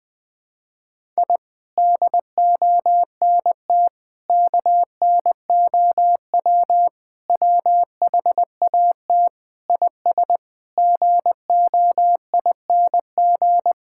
Morse code, heard as a keyed tone, about 20 wpm, 700 Hz. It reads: I DONT KNOW WHAT IS GOING
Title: Sentences with 7-Word Phrases - Spoken Beginning - 20wpm